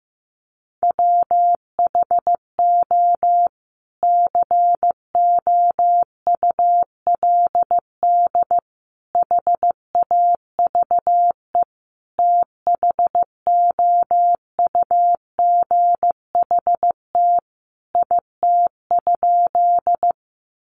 Morse code, heard as a keyed tone, about 15 wpm, 700 Hz. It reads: WHO COULD HAVE THOUGHT IT?
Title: Sentences from Top 300 Words 15wpm